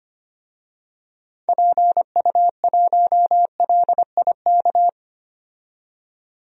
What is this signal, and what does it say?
Morse code, heard as a keyed tone, about 25 wpm, 700 Hz. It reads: PU1LIK
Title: DXCC-100 Call Signs - 6 Characters - 25wpm